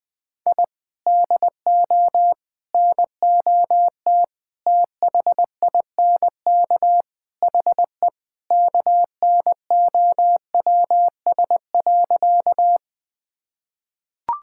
Morse code, heard as a keyed tone, about 20 wpm, 700 Hz. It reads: I DO NOT THINK HE KNOWS.
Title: Sentences with 4-Word Phrases - 20wpm